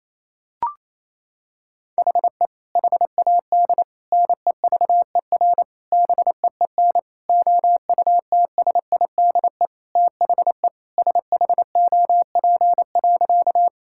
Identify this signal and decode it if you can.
Morse code, heard as a keyed tone, about 28 wpm, 700 Hz. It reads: HE HAD NEVER BEEN OUTSIDE THE SHOP.
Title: Sentences from Top 1600 Words 28wpm